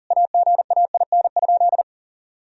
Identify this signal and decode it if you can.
Morse code, heard as a keyed tone, about 37 wpm, 700 Hz. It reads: AGAIN?
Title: Sentences from Top 1000 Words - Review Part 4 - 40wpm